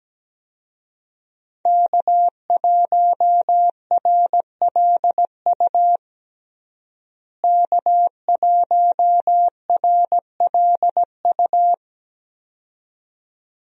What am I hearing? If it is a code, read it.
Morse code, heard as a keyed tone, about 17 wpm, 700 Hz. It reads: K1RLU K1RLU
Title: USA Call Signs - 5 Characters - Repeated 2x - 17wpm